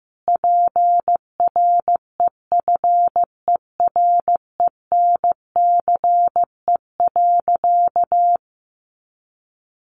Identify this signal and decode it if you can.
Morse code, heard as a keyed tone, about 15 wpm, 700 Hz. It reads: PREFERENCE.
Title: Sentences from Top 1500 Words 15wpm